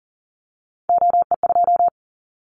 Morse code, heard as a keyed tone, about 40 wpm, 700 Hz. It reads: OE2
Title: Callsign Prefixes for Top 25 DXCC - 40wpm